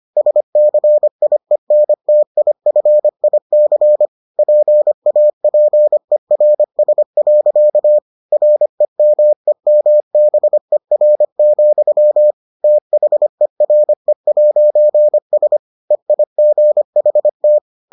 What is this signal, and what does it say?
Morse code, heard as a keyed tone, about 25 wpm, 600 Hz. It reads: SCIENTIFIC PAPERS. REMEMBER, THERE'S EIGHT